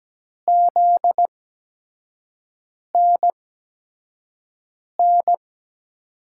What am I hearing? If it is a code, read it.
Morse code, heard as a keyed tone, about 17 wpm, spaced as for 5 wpm, 700 Hz. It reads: ZNN
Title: USA Call Signs - 6 Characters - 8x Character Spacing - 17wpm